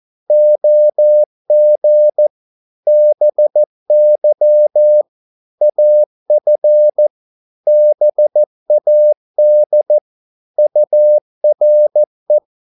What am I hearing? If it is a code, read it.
Morse code, heard as a keyed tone, about 14 wpm, 600 Hz. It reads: OG BY AF BAD URE
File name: Korte ord DK 14wpm